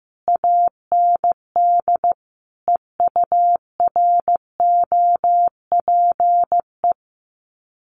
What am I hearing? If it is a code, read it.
Morse code, heard as a keyed tone, about 15 wpm, 700 Hz. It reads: AND EUROPE
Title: Sets of 5 Words Encore 15wpm